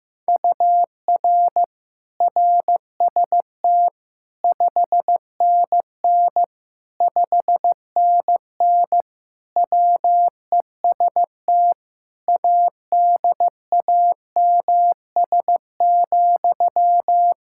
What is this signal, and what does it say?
Morse code, heard as a keyed tone, about 15 wpm, 700 Hz. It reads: UR RST 5NN 5NN WEST ADAMS,
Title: Practice QSO - Exchange 1 - Beginning - 15wpm